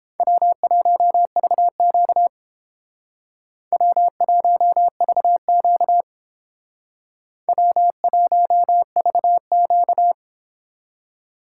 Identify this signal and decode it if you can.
Morse code, heard as a keyed tone, about 28 wpm, 700 Hz. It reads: W1VQ W1VQ W1VQ